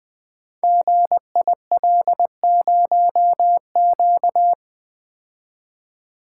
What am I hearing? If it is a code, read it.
Morse code, heard as a keyed tone, about 20 wpm, 700 Hz. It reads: GIL0Q
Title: Code Groups - 5 Letter:Numbers - 20wpm